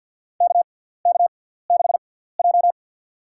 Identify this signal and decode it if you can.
Morse code, heard as a keyed tone, about 50 wpm, spaced as for 17 wpm, 700 Hz. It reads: KK6J
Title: USA Call Signs - 6x Character Spacing - 50wpm